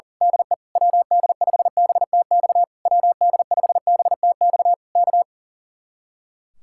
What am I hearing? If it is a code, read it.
Morse code, heard as a keyed tone, about 38 wpm, 700 Hz. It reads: DE WD5BTX WD5BTX K
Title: Speed-Racing Common QSO Phrases Encore - 40wpm